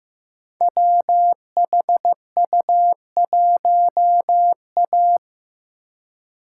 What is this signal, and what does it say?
Morse code, heard as a keyed tone, about 15 wpm, 700 Hz. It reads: WHU1A
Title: Code Groups - 5 Letter:Numbers - 15wpm